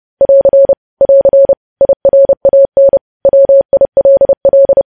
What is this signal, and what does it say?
Morse code, heard as a keyed tone, about 30 wpm, 550 Hz. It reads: <AR> <AR> IRAN WILL
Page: news headlines in morse code at 30 wpm